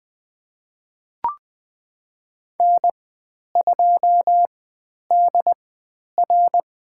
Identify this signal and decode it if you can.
Morse code, heard as a keyed tone, about 20 wpm, spaced as for 10 wpm, 700 Hz. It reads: N2DR
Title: 095 Lesson 16g - Letter c - Call Signs 20wpm